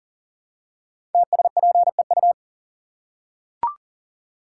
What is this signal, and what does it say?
Morse code, heard as a keyed tone, about 39 wpm, 700 Hz. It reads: TSPEU
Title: Code Groups - 5 Letter:Numbers - 40wpm